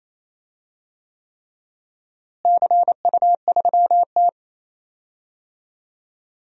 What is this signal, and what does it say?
Morse code, heard as a keyed tone, about 28 wpm, 700 Hz. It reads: CU3T
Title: DXCC-100 Call Signs - 4 Characters - Repeated 3x - 28wpm